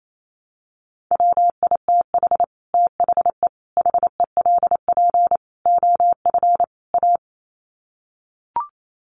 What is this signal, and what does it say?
Morse code, heard as a keyed tone, about 28 wpm, 700 Hz. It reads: WITH THE HELP OF A
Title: Sets of 5 Words 28wpm